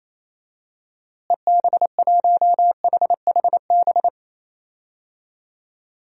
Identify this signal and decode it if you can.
Morse code, heard as a keyed tone, about 28 wpm, 700 Hz. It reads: EB1HHB
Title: DXCC-100 Call Signs - 6 Characters - Repeated 2x - 28wpm